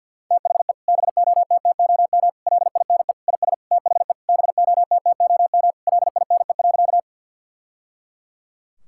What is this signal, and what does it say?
Morse code, heard as a keyed tone, about 51 wpm, 700 Hz. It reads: THE BOTTOM LINE IS THE BOTTOM LINE.
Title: Sentences from Top 1000 Words - Review Part 6 - 50wpm